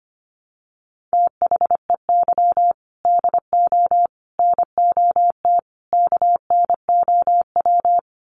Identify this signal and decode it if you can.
Morse code, heard as a keyed tone, about 25 wpm, 700 Hz. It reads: THEY DO NOT KNOW
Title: Sets of 4 Words 25wpm